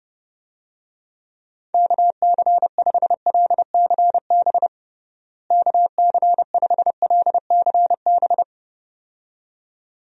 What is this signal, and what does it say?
Morse code, heard as a keyed tone, about 30 wpm, 700 Hz. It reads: KC5LCB KC5LCB
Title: USA Call Signs - 6 Characters - Repeated 2x - 30wpm